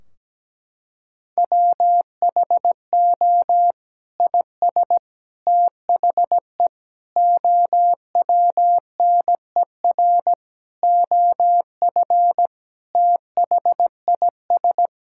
Morse code, heard as a keyed tone, about 17 wpm, 700 Hz. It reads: WHO IS THE OWNER OF THIS
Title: Sentences from Top 1600 Words 17wpm